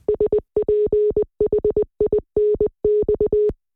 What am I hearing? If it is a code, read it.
Morse code, heard as a keyed tone, about 20 wpm, 420 Hz. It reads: SPHINX